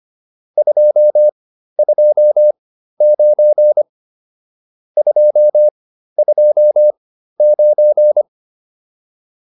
Morse code, heard as a keyed tone, about 25 wpm, spaced as for 13 wpm, 600 Hz. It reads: 229 229
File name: QSO402 13